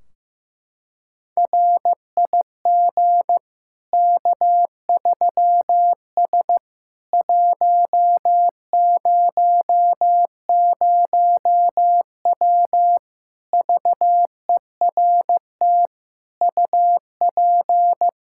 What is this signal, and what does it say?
Morse code, heard as a keyed tone, about 16 wpm, 700 Hz. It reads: RIG K3S 100W VERT UP
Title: K1USN SST QSO Practice 15wpm